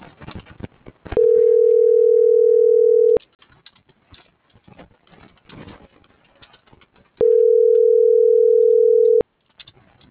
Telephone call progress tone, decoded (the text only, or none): ringback tone